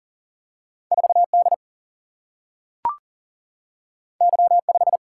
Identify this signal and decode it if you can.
Morse code, heard as a keyed tone, about 41 wpm, 700 Hz. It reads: 4D Y5
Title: Single-Letter Single-Number 40wpm